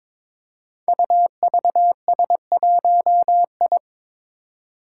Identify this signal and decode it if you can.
Morse code, heard as a keyed tone, about 22 wpm, 700 Hz. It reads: UVS1I
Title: Five Challenging Letter-Number Combinations 22wpm